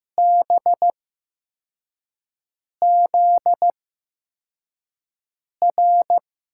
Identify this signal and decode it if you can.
Morse code, heard as a keyed tone, about 15 wpm, spaced as for 4 wpm, 700 Hz. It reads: BZR
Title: USA Call Signs - 5 Characters - 8x Character Spacing - 15wpm